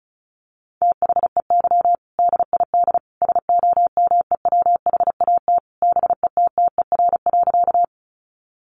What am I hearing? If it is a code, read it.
Morse code, heard as a keyed tone, about 34 wpm, 700 Hz. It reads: THEY DID SOMEWHAT BETTER.
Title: Sentences from Top 1300 Words 35wpm